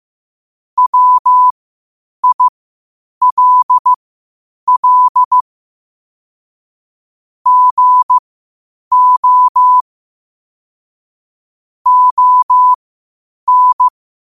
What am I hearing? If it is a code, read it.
Morse code, heard as a keyed tone, about 15 wpm, spaced as for 8 wpm, 1000 Hz. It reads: WILL GO ON
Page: Quotes for Fri, 15 Aug 2025 in Morse Code at 8 words per minute.